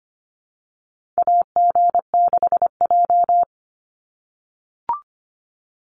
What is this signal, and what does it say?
Morse code, heard as a keyed tone, about 25 wpm, 690 Hz. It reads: AG6J
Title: USA Call Signs 25wpm